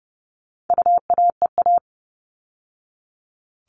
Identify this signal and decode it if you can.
Morse code, heard as a keyed tone, about 30 wpm, 700 Hz. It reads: UAEA
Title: All Cut Numbers - 4-Digit 30wpm